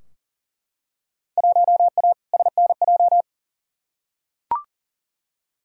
Morse code, heard as a keyed tone, about 40 wpm, 700 Hz. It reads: 1A SNJ
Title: Field Day Exchanges - 40wpm